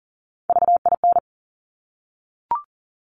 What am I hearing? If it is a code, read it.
Morse code, heard as a keyed tone, about 38 wpm, 700 Hz. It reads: VIN
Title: Top Three Letter Combinations in Words 40wpm